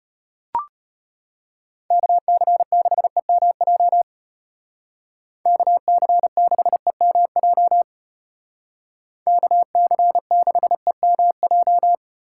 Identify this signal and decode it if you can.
Morse code, heard as a keyed tone, about 31 wpm, 700 Hz. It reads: KC6EMJ KC6EMJ KC6EMJ
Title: Speed Racing - USA 2x3 Call Signs 25wpm